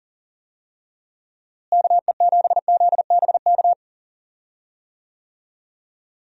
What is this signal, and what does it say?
Morse code, heard as a keyed tone, about 40 wpm, 700 Hz. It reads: KE7ZBK
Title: USA Call Signs - 6 Characters - Repeated 3x - 40wpm